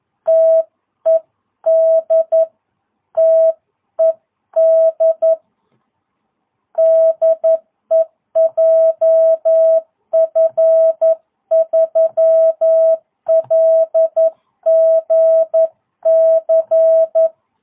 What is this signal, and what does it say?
Morse code, heard as a keyed tone, about 11 wpm, 650 Hz. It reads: TED TED DE JF3LGC
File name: jh8cbh / 2022cw